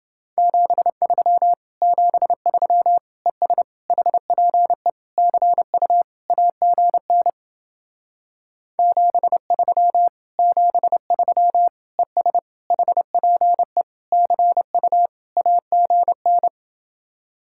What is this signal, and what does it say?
Morse code, heard as a keyed tone, about 30 wpm, 700 Hz. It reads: 73 73 ES HPE CU AGN 73 73 ES HPE CU AGN